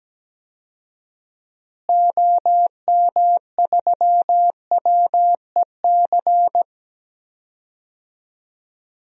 Morse code, heard as a keyed tone, about 17 wpm, 700 Hz. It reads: OM3WEC
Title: DXCC-100 Call Signs - 6 Characters - Repeated 2x - 17wpm